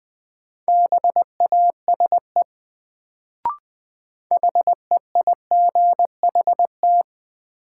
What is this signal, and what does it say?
Morse code, heard as a keyed tone, about 20 wpm, 700 Hz. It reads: BASE HEIGHT
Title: Sets of 5 Words - New Word Drill - 20wpm